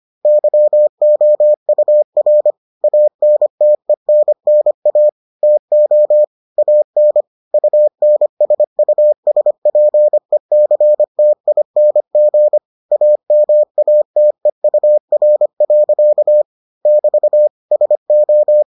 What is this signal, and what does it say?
Morse code, heard as a keyed tone, about 25 wpm, 600 Hz. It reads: YOUR ANTENNA TO AN UNSUSPECTING AMATEUR. <BT> SO